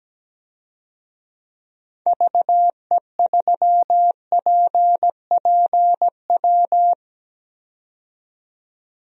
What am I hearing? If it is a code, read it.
Morse code, heard as a keyed tone, about 17 wpm, 700 Hz. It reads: VE3PPW
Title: DXCC-100 Call Signs - 6 Characters - 17wpm